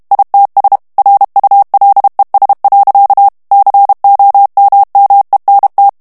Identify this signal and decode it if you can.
Morse code, heard as a keyed tone, about 32 wpm, 800 Hz. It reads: ITS RULES. COMMENT
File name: morse mp3 / ANGLAIS texte clair 3 vitesse 30 mots minute